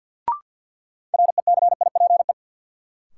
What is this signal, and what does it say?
Morse code, heard as a keyed tone, about 52 wpm, 700 Hz. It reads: RECIPE
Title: Sentences from Top 1200 Words - New Word Drill - 50wpm